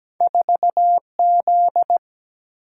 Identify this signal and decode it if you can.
Morse code, heard as a keyed tone, about 17 wpm, 700 Hz. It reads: 4Z
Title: Single-Letter Single-Number 17wpm